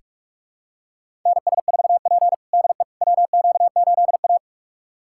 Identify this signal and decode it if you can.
Morse code, heard as a keyed tone, about 44 wpm, 690 Hz. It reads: NS4P DE WQ8A
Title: USA Callsign DE Callsign - 4 Characters - 45wpm